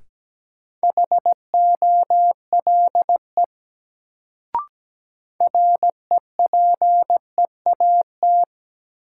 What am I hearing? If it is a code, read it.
Morse code, heard as a keyed tone, about 17 wpm, 700 Hz. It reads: HOLE REPEAT
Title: Sentences from Top 1600 Words - New Word Drill - 17wpm